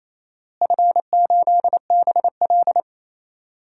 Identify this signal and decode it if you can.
Morse code, heard as a keyed tone, about 28 wpm, 700 Hz. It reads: F8BL
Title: DXCC-100 Call Signs - 4 Characters - Repeated 2x - 28wpm